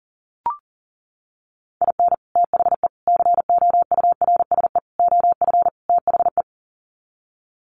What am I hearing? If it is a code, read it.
Morse code, heard as a keyed tone, about 40 wpm, 700 Hz. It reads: IN THE COURSE OF THE
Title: Sets of 5 Words 40wpm